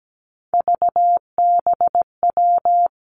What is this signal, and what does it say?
Morse code, heard as a keyed tone, about 17 wpm, 700 Hz. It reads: VBW
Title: Three Challenging Letter-Number Combinations 17wpm